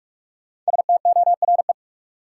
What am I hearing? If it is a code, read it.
Morse code, heard as a keyed tone, about 47 wpm, 700 Hz. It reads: STORE
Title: Sentences from Top 1000 Words - Review Part 5 - New Word Drill - 45wpm